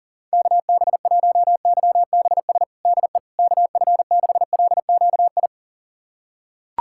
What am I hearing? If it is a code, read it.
Morse code, heard as a keyed tone, about 40 wpm, 700 Hz. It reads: KB1YBS DE KF6LQI
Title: USA Callsign DE Callsign - 6 Characters - 40wpm